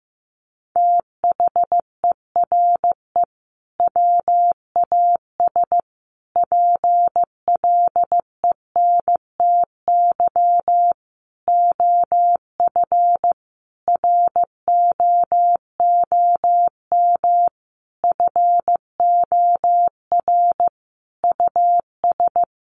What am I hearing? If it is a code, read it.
Morse code, heard as a keyed tone, about 15 wpm, 700 Hz. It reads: THERE WAS PLENTY OF ROOM FOR US
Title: Sentences from Top 1300 Words 15wpm